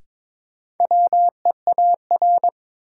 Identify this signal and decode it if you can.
Morse code, heard as a keyed tone, about 22 wpm, 700 Hz. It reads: WEAR
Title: Sentences from Top 1500 Words - New Word Drill Alt - 22wpm